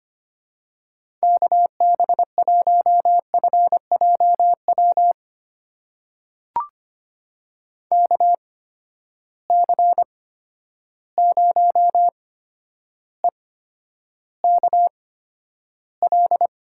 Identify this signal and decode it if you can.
Morse code, heard as a keyed tone, about 25 wpm, spaced as for 11 wpm, 700 Hz. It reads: KB1FJW KC0EKL
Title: USA Call Signs - 6 Characters - 8x Character Spacing - 25wpm